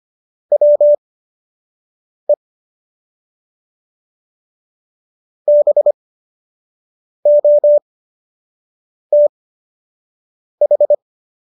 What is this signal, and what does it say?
Morse code, heard as a keyed tone, about 25 wpm, spaced as for 6 wpm, 590 Hz. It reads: WE BOTH